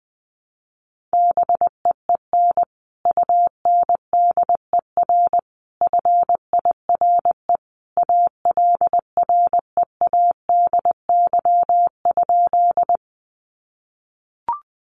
Morse code, heard as a keyed tone, about 20 wpm, 700 Hz. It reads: BEEN UNDER FIRE ALREADY?
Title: Sentences from Top 700 Words 20wpm